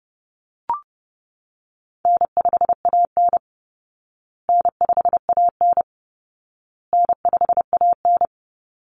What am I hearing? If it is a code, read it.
Morse code, heard as a keyed tone, about 30 wpm, 700 Hz. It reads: N5AN N5AN N5AN
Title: Call Signs Repeated 2x - 30wpm